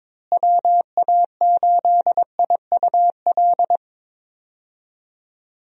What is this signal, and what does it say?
Morse code, heard as a keyed tone, about 22 wpm, 700 Hz. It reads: WA8IUL
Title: USA Call Signs - 6 Characters - Repeated 3x - 22wpm